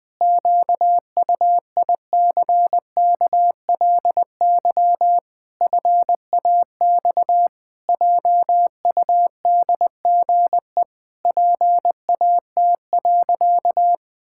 Morse code, heard as a keyed tone, about 20 wpm, 700 Hz. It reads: QUICKLY FAX JUDGE PAT.
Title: Pangram Challenge v2 - 20wpm